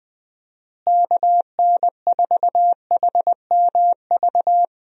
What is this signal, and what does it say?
Morse code, heard as a keyed tone, about 20 wpm, 700 Hz. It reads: KN4HMV
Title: USA Call Signs - 6 Characters - Repeated 3x - 20wpm